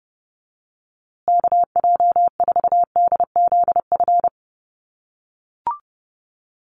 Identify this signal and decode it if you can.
Morse code, heard as a keyed tone, about 30 wpm, 700 Hz. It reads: KJ4DZF
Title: USA Call Signs Repeated 2x 30wpm